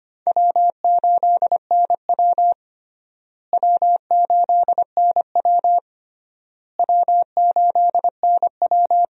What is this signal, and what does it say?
Morse code, heard as a keyed tone, about 25 wpm, 700 Hz. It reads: W8NW W8NW W8NW
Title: USA Call Signs - 4 Characters - Repeated 3x - 25wpm